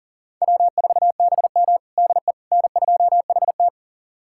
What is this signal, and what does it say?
Morse code, heard as a keyed tone, about 37 wpm, 700 Hz. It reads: W4BM DE N2HT
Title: USA Callsign DE Callsign - 4 Characters - 40wpm